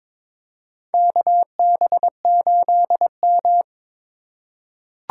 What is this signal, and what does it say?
Morse code, heard as a keyed tone, about 21 wpm, 700 Hz. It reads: KB8M
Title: K1USN SST QSO Practice - 22wpm